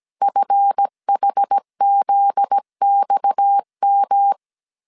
Morse code, heard as a keyed tone, about 17 wpm, 800 Hz. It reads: FHZXM
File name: CW / DET / 1 9 1500C S C 16wpm 2909